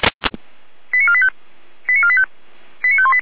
Five-tone selective call (ZVEI), heard as quick runs of five tones, tones of about 70 ms. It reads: 89485 89485 89518